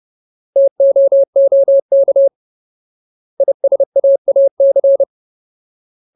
Morse code, heard as a keyed tone, about 30 wpm, 550 Hz. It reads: TOOK ISAAC
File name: Ultimate Word List / 30WPM Ultimate